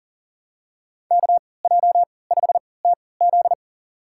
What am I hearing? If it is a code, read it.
Morse code, heard as a keyed tone, about 40 wpm, spaced as for 23 wpm, 700 Hz. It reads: KJ5TZ
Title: USA Call Signs - 5 Characters - 3x Character Spacing - 40wpm